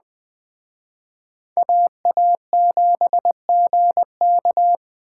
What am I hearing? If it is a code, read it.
Morse code, heard as a keyed tone, about 20 wpm, 700 Hz. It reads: AA7GK
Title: Speed Racing - USA 2x2 Call Signs 17wpm